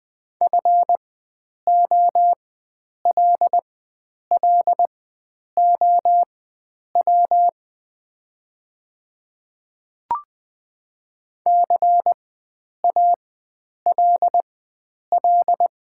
Morse code, heard as a keyed tone, about 20 wpm, spaced as for 9 wpm, 700 Hz. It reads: FOLLOW CALL
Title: 4x Character Spacing - Top 500 Words - 20wpm